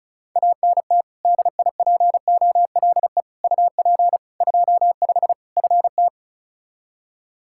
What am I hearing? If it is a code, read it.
Morse code, heard as a keyed tone, about 36 wpm, 700 Hz. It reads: ANT DIPOLE UP 25 FT
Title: Practice QSO - Exchange 2 - Info - 35wpm